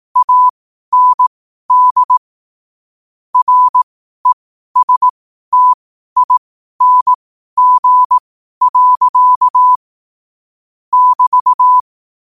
Morse code, heard as a keyed tone, about 18 wpm, spaced as for 12 wpm, 1000 Hz. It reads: AND RESTING. <BT>
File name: 12wpm